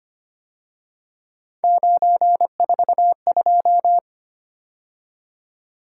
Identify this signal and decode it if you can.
Morse code, heard as a keyed tone, about 31 wpm, 700 Hz. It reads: 942
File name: Speed-Racing - Three Letter-Number Combinations - 25wpm